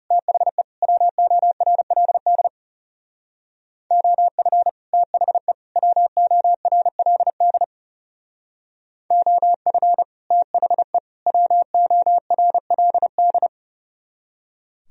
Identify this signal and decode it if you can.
Morse code, heard as a keyed tone, about 37 wpm, 700 Hz. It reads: THE WORLD OF THE WORLD OF THE WORLD
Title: Speed-Racing - Sets of 3 Words 30wpm